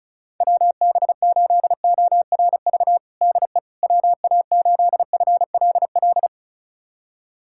Morse code, heard as a keyed tone, about 34 wpm, 700 Hz. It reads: WB8ORV DE WA8FLL
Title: USA Callsign DE Callsign - 6 Characters - 35wpm